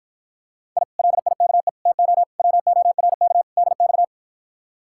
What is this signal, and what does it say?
Morse code, heard as a keyed tone, about 53 wpm, 700 Hz. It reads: I LIKE TO WORK DX